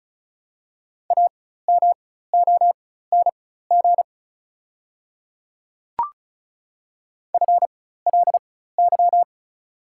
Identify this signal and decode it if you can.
Morse code, heard as a keyed tone, about 34 wpm, spaced as for 16 wpm, 690 Hz. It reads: AMONG FLY
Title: 4x Character Spacing - Top 500 Words - 35wpm